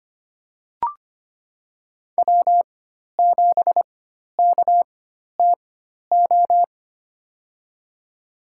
Morse code, heard as a keyed tone, about 25 wpm, spaced as for 12 wpm, 700 Hz. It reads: W7KTO
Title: USA Call Signs - 5 Characters - 4x Character Spacing - 25wpm